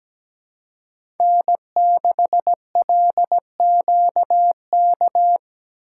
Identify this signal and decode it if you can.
Morse code, heard as a keyed tone, about 17 wpm, 700 Hz. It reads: N6LQK
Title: USA Call Signs - 5 Characters - 2x Character Spacing - 17wpm